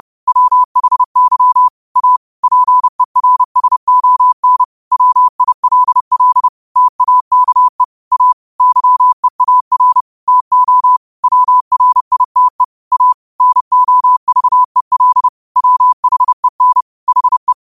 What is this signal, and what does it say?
Morse code, heard as a keyed tone, about 30 wpm, 1000 Hz. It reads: WHY A PERSON WILL TAKE A YEAR TO WRITE A NOVEL WHEN HE